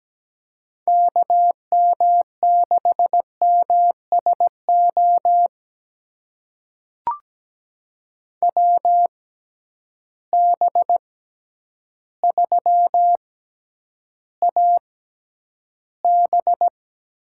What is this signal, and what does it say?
Morse code, heard as a keyed tone, about 17 wpm, spaced as for 9 wpm, 700 Hz. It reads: KM6MSO WB3AB
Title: USA Call Signs - 6 Characters - 6x Character Spacing - 17wpm